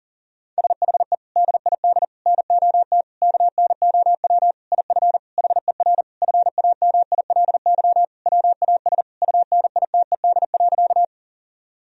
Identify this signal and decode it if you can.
Morse code, heard as a keyed tone, about 40 wpm, 700 Hz. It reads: SHE DID NOT KNOW IF HER FAMILY WAS UNITED.
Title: Sentences from Top 1000 Words - Review Part 4 - 40wpm